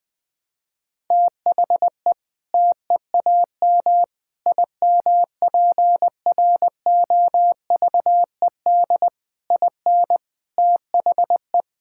Morse code, heard as a keyed tone, about 20 wpm, 700 Hz. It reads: THE TEAM IMPROVED IN THE
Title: Sentences with 7-Word Phrases - Spoken Beginning - 20wpm